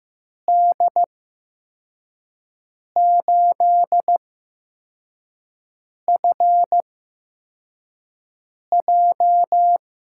Morse code, heard as a keyed tone, about 15 wpm, spaced as for 4 wpm, 700 Hz. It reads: D8FJ
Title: USA Call Signs - 5 Characters - 8x Character Spacing - 15wpm